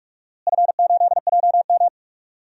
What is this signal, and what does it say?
Morse code, heard as a keyed tone, about 46 wpm, 700 Hz. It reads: F8JM
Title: DXCC-100 Call Signs - 4 Characters - Repeated 2x - 45wpm